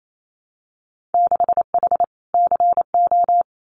Morse code, heard as a keyed tone, about 28 wpm, 700 Hz. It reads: 6H CO
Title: Winter Field Day Exchanges Encore 28wpm